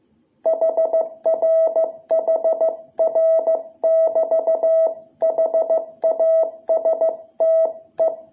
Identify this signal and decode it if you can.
Morse code, heard as a keyed tone, about 15 wpm, 660 Hz. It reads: HR HR <BT> HASTE
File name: jh8cbh / 2021cw